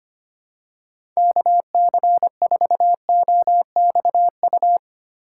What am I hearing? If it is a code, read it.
Morse code, heard as a keyed tone, about 25 wpm, 700 Hz. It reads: KC4OXU
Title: USA Call Signs - 2x Character Spacing - 25wpm